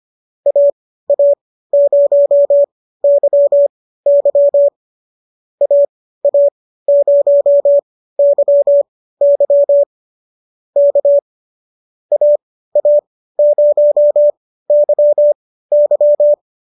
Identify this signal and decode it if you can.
Morse code, heard as a keyed tone, about 25 wpm, spaced as for 15 wpm, 590 Hz. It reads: AA0YY AA0YY K AA0YY
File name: qso213 15